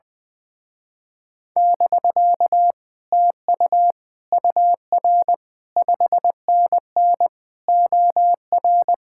Morse code, heard as a keyed tone, about 20 wpm, 700 Hz. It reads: <BK> TU UR 5NN OR
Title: Speed-Racing Common QSO Phrases - 20wpm